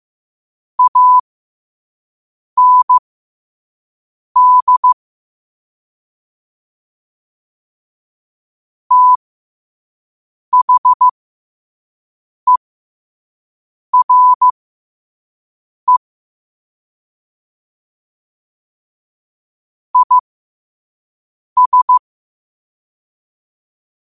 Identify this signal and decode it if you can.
Morse code, heard as a keyed tone, about 15 wpm, spaced as for 5 wpm, 1000 Hz. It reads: AND THERE IS